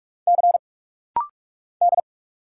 Morse code, heard as a keyed tone, about 44 wpm, 700 Hz. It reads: C D
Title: Single Letter-Number Alt - Rapid-Fire 45wpm